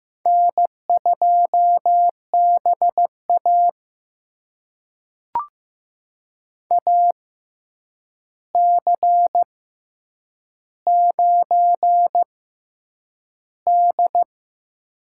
Morse code, heard as a keyed tone, about 15 wpm, spaced as for 8 wpm, 700 Hz. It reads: N2BA AC9D
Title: USA Call Signs - 4 Characters - 6x Character Spacing - 15wpm